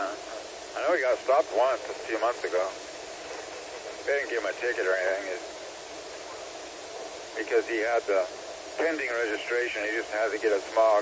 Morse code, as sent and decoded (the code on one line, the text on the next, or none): none